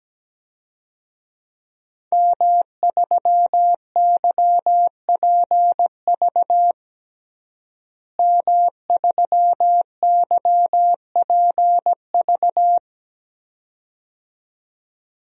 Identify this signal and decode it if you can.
Morse code, heard as a keyed tone, about 17 wpm, 700 Hz. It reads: M3YPV M3YPV
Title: DXCC-100 Call Signs - 5 Characters - Repeated 2x - 17wpm